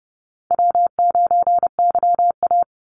Morse code, heard as a keyed tone, about 30 wpm, 700 Hz. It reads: W9YA
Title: Easy Call Signs - 30wpm